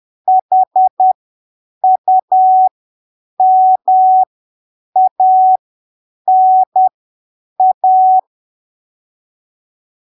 Morse code, heard as a keyed tone, about 10 wpm, spaced as for 7 wpm, 750 Hz. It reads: HUMANA
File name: arquivos / texto / texto 05 10ppm